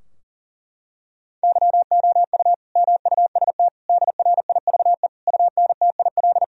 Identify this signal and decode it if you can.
Morse code, heard as a keyed tone, about 37 wpm, 700 Hz. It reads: YOU MUST DRIVE UNTIL
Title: Sentences from Top 1000 Words - Review Part 2 - 40wpm